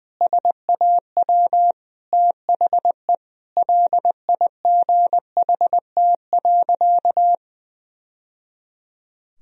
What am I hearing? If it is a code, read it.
Morse code, heard as a keyed tone, about 20 wpm, 700 Hz. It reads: SAW THE LIGHT.
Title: Sentences from Top 1000 Words - Review Part 1 - 20wpm